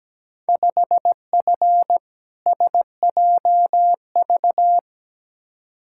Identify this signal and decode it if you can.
Morse code, heard as a keyed tone, about 17 wpm, 700 Hz. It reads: 5F SJV
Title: Field Day Exchanges Encore - 17wpm